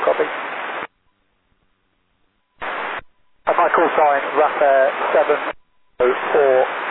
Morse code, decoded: none

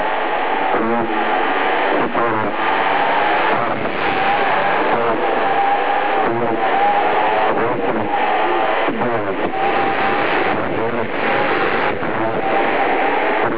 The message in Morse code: none

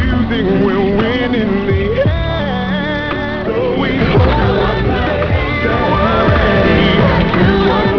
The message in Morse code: none

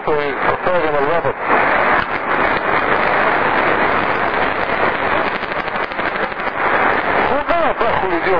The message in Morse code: none

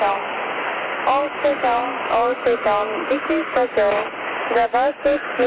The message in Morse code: none

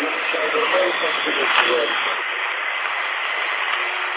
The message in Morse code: none

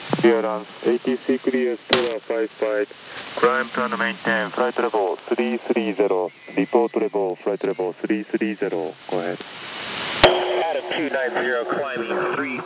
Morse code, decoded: none

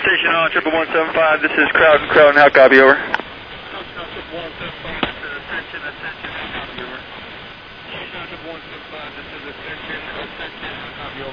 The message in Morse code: none

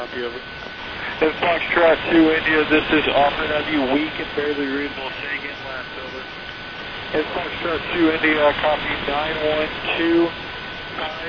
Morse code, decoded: none

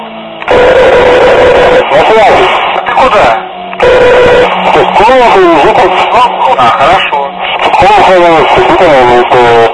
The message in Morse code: none

